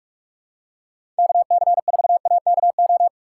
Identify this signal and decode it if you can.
Morse code, heard as a keyed tone, about 48 wpm, 700 Hz. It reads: KC4AKO
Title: USA Call Signs - 6x Character Spacing - 45wpm